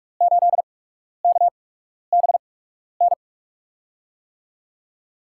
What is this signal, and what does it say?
Morse code, heard as a keyed tone, about 44 wpm, spaced as for 12 wpm, 700 Hz. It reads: 8KBN
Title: USA Call Signs - 6 Characters - 8x Character Spacing - 45wpm